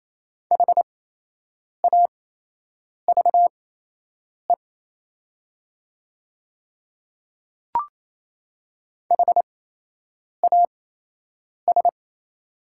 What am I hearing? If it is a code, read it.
Morse code, heard as a keyed tone, about 28 wpm, spaced as for 8 wpm, 700 Hz. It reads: HAVE HAS
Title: 8x Character Spacing - Top 100 Words - 28wpm